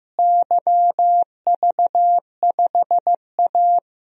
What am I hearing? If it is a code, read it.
Morse code, heard as a keyed tone, about 15 wpm, 700 Hz. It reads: YV5A
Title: DXCC-100 Call Signs - 4 Characters - Repeated 2x - 15wpm